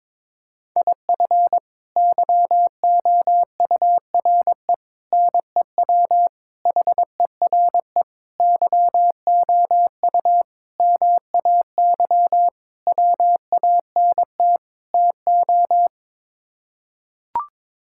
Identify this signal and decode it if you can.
Morse code, heard as a keyed tone, about 22 wpm, 700 Hz. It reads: IF YOURE NEW HERE YOU MAY WANT TO
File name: practice / 22wpm / Sets of 8 Words - 22wpm